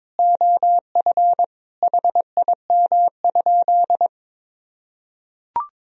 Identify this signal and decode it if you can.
Morse code, heard as a keyed tone, about 22 wpm, 700 Hz. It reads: OF HIM?
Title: Sentences from Top 100 Words and Abbreviations 22wpm